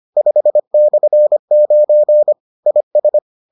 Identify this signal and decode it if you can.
Morse code, heard as a keyed tone, about 25 wpm, 600 Hz. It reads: 5/9 IS